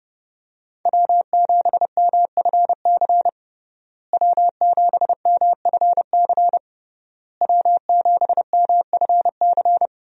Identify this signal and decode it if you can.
Morse code, heard as a keyed tone, about 32 wpm, 700 Hz. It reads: W7MFC W7MFC W7MFC
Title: USA Call Signs - 5 Characters - Repeated 3x - 30wpm